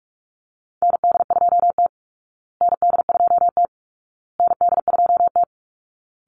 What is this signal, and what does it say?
Morse code, heard as a keyed tone, about 44 wpm, 700 Hz. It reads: ND2T ND2T ND2T
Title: Call Signs Repeated 2x - 45wpm